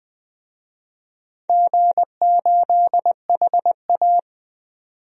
Morse code, heard as a keyed tone, about 20 wpm, 700 Hz. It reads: G8HA
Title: DXCC-100 Call Signs - 4 Characters - 20wpm